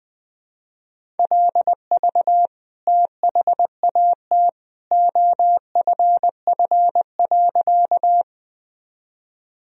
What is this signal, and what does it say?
Morse code, heard as a keyed tone, about 20 wpm, 700 Hz. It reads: LV THAT OFF.
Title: Sentences from Top 400 Words and Abbreviations 20wpm